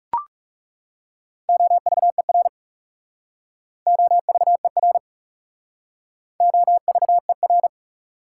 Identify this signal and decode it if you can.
Morse code, heard as a keyed tone, about 36 wpm, 690 Hz. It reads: OVER OVER OVER
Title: Speed-Racing Prefixes and Suffixes 30wpm